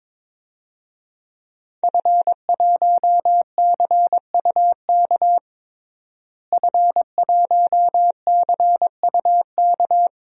Morse code, heard as a keyed tone, about 22 wpm, 700 Hz. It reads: F1CUK F1CUK
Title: DXCC-100 Call Signs - 5 Characters - Repeated 3x - 22wpm